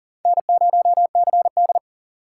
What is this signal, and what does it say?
Morse code, heard as a keyed tone, about 40 wpm, 700 Hz. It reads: N0CD
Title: USA Call Signs - 4 Characters - 6x Character Spacing - 40wpm